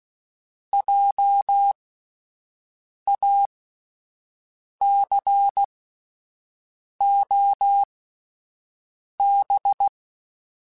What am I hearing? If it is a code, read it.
Morse code, heard as a keyed tone, about 16 wpm, spaced as for 5 wpm, 800 Hz. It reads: JACOB